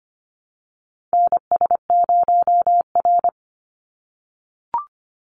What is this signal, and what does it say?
Morse code, heard as a keyed tone, about 25 wpm, 700 Hz. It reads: NS0R
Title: Call Signs - 25wpm